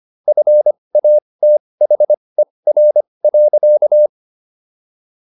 Morse code, heard as a keyed tone, about 25 wpm, spaced as for 20 wpm, 600 Hz. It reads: FATHER.